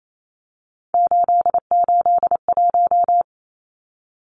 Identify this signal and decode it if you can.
Morse code, heard as a keyed tone, about 28 wpm, 700 Hz. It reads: 881
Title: Numbers - 3-Digit 28wpm